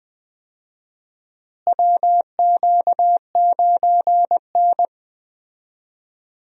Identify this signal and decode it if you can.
Morse code, heard as a keyed tone, about 21 wpm, 700 Hz. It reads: WQ9N
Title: USA Call Signs - 4 Characters - Repeated 2x - 20wpm